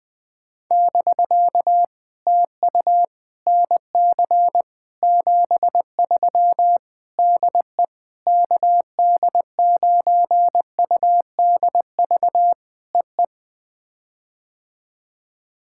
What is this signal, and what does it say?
Morse code, heard as a keyed tone, about 20 wpm, 700 Hz. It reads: <BK> TU NC 73 DE KD9UDV EE
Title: POTA QSO Practice - 20wpm